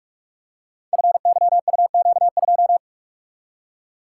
Morse code, heard as a keyed tone, about 44 wpm, 700 Hz. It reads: FYUQ2
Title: Code Groups - 5 Letter:Numbers - 45wpm